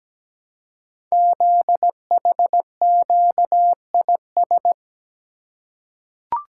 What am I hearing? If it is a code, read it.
Morse code, heard as a keyed tone, about 17 wpm, 700 Hz. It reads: ZHQIS
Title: Code Groups - 5 Letters - 17wpm